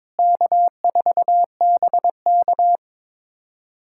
Morse code, heard as a keyed tone, about 22 wpm, 700 Hz. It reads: K4BK
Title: USA Callsign DE Callsign - 4 Characters - 22wpm